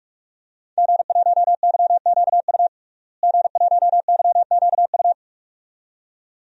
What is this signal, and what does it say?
Morse code, heard as a keyed tone, about 44 wpm, 700 Hz. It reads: G1YQU G1YQU
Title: DXCC-100 Call Signs - 5 Characters - Repeated 2x - 45wpm